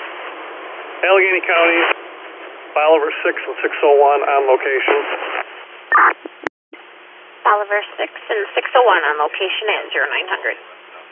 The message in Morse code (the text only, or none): none